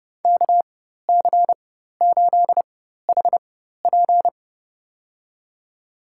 Morse code, heard as a keyed tone, about 30 wpm, spaced as for 14 wpm, 700 Hz. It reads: KC8HP
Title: USA Call Signs - 4x Character Spacing - 30wpm